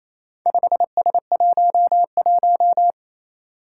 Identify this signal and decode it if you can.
Morse code, heard as a keyed tone, about 28 wpm, 700 Hz. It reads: 5S11
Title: Four Challenging Letter-Number Combinations 28wpm